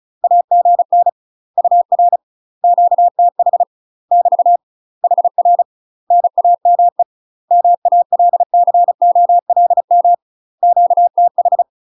Morse code, heard as a keyed tone, about 36 wpm, 700 Hz. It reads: AGN UR QTH <BT> HR NAME MALCOLM QTH
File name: UK MASTER QSOs TRAINING / 35WPM / UK MASTER QSOs 0010